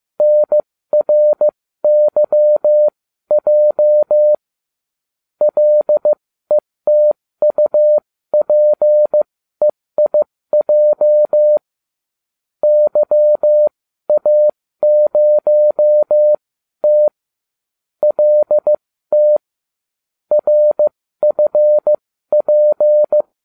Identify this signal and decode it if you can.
Morse code, heard as a keyed tone, about 15 wpm, spaced as for 13 wpm, 600 Hz. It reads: NRYJ LETUPEIJ YA0T LT RFP